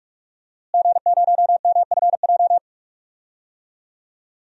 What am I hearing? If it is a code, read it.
Morse code, heard as a keyed tone, about 46 wpm, 700 Hz. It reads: G0MFJ
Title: DXCC-100 Call Signs - 5 Characters - Repeated 2x - 45wpm